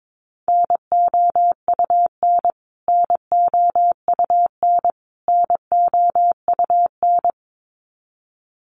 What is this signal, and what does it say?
Morse code, heard as a keyed tone, about 22 wpm, 700 Hz. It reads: NOUN NOUN NOUN
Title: Top 500 Words Repeated 22wpm